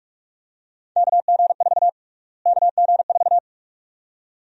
Morse code, heard as a keyed tone, about 46 wpm, 700 Hz. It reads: KG4 KG4
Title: USA Call Sign Prefixes - Repeated 2x - 45wpm